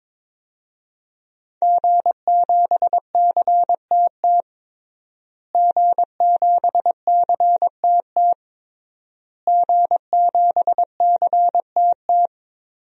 Morse code, heard as a keyed tone, about 22 wpm, 700 Hz. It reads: G7CTT G7CTT G7CTT
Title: DXCC-100 Call Signs - 5 Characters - Repeated 3x - 22wpm